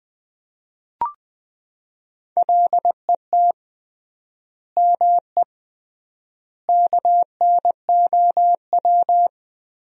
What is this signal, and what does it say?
Morse code, heard as a keyed tone, about 20 wpm, 700 Hz. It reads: LET ME KNOW
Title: 3x Word-Spacing - Sentences from Top 300 Words - 20wpm